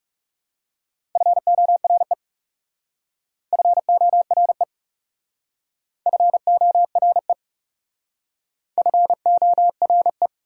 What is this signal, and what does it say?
Morse code, heard as a keyed tone, about 40 wpm, 690 Hz. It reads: FORE FORE FORE FORE